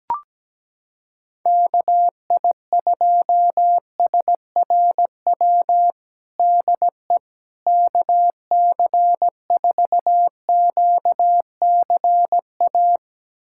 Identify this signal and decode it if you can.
Morse code, heard as a keyed tone, about 17 wpm, 700 Hz. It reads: KI2SRW DE KC4QCA
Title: USA Callsign DE Callsign 17wpm